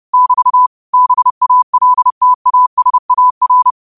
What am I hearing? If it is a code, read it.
Morse code, heard as a keyed tone, about 30 wpm, 1000 Hz. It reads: <BT> BALTASAR
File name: qotdfiles / 30wpm